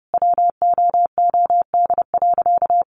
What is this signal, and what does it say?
Morse code, heard as a keyed tone, about 30 wpm, 700 Hz. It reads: WOOD.
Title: Sentences from Top 1300 Words 30wpm